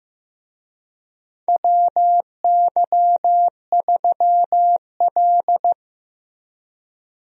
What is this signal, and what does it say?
Morse code, heard as a keyed tone, about 16 wpm, 700 Hz. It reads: WY3L
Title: USA Call Signs - 4 Characters - Repeated 3x - 15wpm